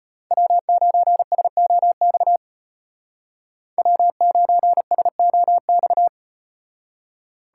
Morse code, heard as a keyed tone, about 38 wpm, 700 Hz. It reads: W9SOX W9SOX